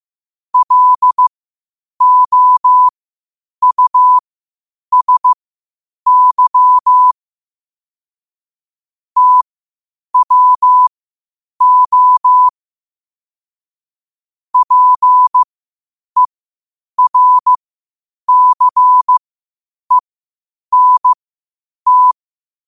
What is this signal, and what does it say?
Morse code, heard as a keyed tone, about 15 wpm, spaced as for 8 wpm, 1000 Hz. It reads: LOUSY TWO PERCENT
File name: qotdfiles / 8wpm